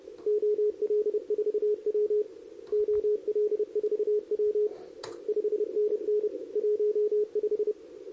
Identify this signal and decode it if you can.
Morse code, heard as a keyed tone, about 29 wpm, 410 Hz. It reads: OL4W OL4W 5NN 15
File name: CQWWCW2023 / SP